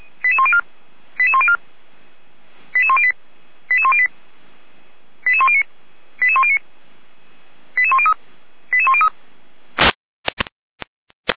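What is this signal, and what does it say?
Five-tone selective call (ZVEI), heard as quick runs of five tones, tones of about 70 ms. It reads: 88104 88104 88108 88108 88109 88109 88103 88103
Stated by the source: ZVEI five-tone selective call received by SDR.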